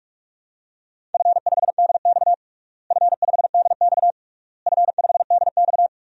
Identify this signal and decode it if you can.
Morse code, heard as a keyed tone, about 46 wpm, 700 Hz. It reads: F5DX F5DX F5DX
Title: DXCC-100 Call Signs - 4 Characters - Repeated 3x - 45wpm